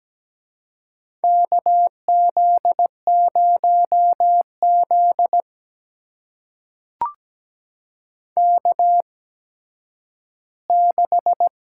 Morse code, heard as a keyed tone, about 17 wpm, spaced as for 10 wpm, 700 Hz. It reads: KZ0Z K6
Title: USA Call Signs - 4 Characters - 8x Character Spacing - 17wpm